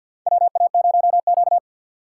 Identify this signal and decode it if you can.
Morse code, heard as a keyed tone, about 51 wpm, 700 Hz. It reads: WA0Q
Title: USA Call Signs - 4 Characters - 2x Character Spacing - 50wpm